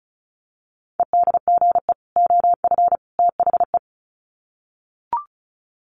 Morse code, heard as a keyed tone, about 34 wpm, 700 Hz. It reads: EDGE OF THE
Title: Sets of 3 Words 35wpm